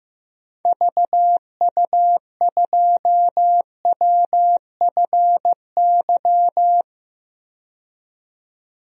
Morse code, heard as a keyed tone, about 16 wpm, 700 Hz. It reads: VU2WFY
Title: DXCC-100 Call Signs - 6 Characters - 15wpm